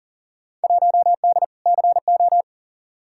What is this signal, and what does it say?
Morse code, heard as a keyed tone, about 38 wpm, 700 Hz. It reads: 1D CO
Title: Field Day Exchanges - 40wpm